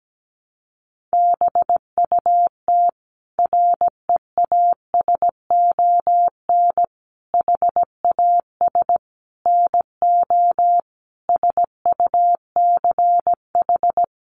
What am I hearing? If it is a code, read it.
Morse code, heard as a keyed tone, about 17 wpm, 700 Hz. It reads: BUT REASON HAS NO SUCH
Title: Sentences from Top 1100 Words 17wpm